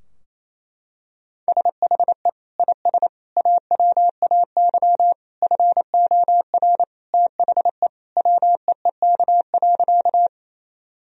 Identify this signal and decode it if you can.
Morse code, heard as a keyed tone, about 27 wpm, 700 Hz. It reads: SHE IS AWAY FOR THE WEEK.
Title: Sentences from Top 1000 Words - Review Part 2 - 28wpm